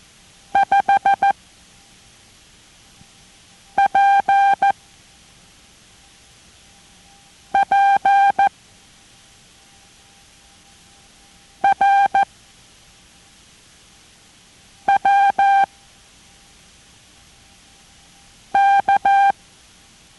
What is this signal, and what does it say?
Morse code, heard as a keyed tone, about 14 wpm, spaced as for 3 wpm, 790 Hz. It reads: 5PPRWK